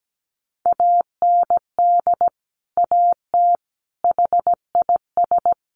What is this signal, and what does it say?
Morse code, heard as a keyed tone, about 17 wpm, 700 Hz. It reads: AND AT HIS
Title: Sentences from Top 1500 Words 17wpm